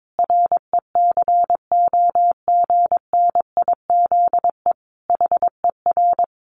Morse code, heard as a keyed tone, about 22 wpm, 700 Hz. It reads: RECOGNIZE HER
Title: Sentences from Top 1400 Words 22wpm